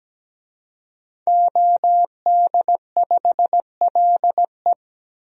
Morse code, heard as a keyed tone, about 17 wpm, 700 Hz. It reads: OD5LE
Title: DXCC-100 Call Signs - 5 Characters - Repeated 2x - 17wpm